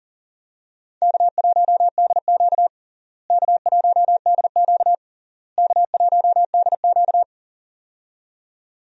Morse code, heard as a keyed tone, about 40 wpm, 700 Hz. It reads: K1DQ K1DQ K1DQ
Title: USA Call Signs - 4 Characters - Repeated 3x - 40wpm